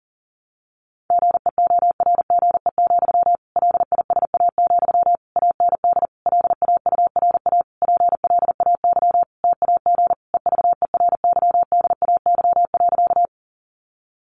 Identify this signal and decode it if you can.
Morse code, heard as a keyed tone, about 40 wpm, 700 Hz. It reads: GEORGE, LISA, AND LAURA PLAY TAG EVERYDAY.
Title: Additional Easy Phrases 40wpm